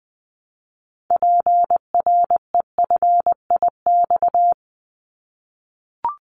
Morse code, heard as a keyed tone, about 20 wpm, 700 Hz. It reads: PREFIX
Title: Callsign Prefixes for Top 25 DXCC - 20wpm